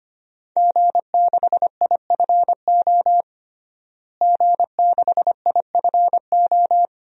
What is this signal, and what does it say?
Morse code, heard as a keyed tone, about 25 wpm, 700 Hz. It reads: G6IFO G6IFO
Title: DXCC-100 Call Signs - 5 Characters - Repeated 3x - 25wpm